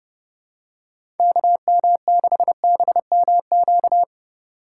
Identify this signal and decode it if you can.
Morse code, heard as a keyed tone, about 30 wpm, 700 Hz. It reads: KM6BMQ
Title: USA Call Signs - 8x Character Spacing - 30wpm